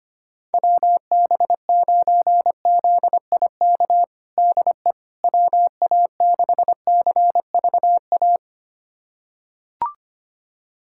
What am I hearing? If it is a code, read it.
Morse code, heard as a keyed tone, about 25 wpm, 700 Hz. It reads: WB9ZIK DE WA6CVA
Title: USA Callsign DE Callsign - 6 Characters - 25wpm